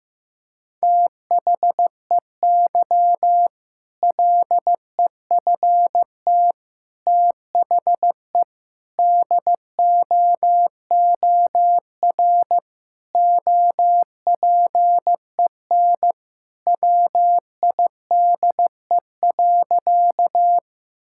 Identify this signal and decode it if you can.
Morse code, heard as a keyed tone, about 15 wpm, 700 Hz. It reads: THEY LEFT THE DOOR OPEN WIDE.
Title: Sentences from Top 1000 Words - Review Part 6 - 15wpm